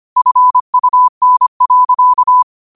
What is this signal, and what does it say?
Morse code, heard as a keyed tone, about 25 wpm, 1000 Hz. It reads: FUN.
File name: qotdfiles / 25wpm